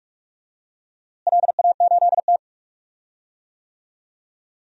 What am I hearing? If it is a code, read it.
Morse code, heard as a keyed tone, about 42 wpm, 700 Hz. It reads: LA8T
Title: DXCC-100 Call Signs - 4 Characters - Repeated 3x - 45wpm